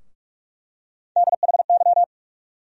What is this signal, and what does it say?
Morse code, heard as a keyed tone, about 45 wpm, 700 Hz. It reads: DHY
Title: USA Call Sign Postfixes - 3 Letters 45wpm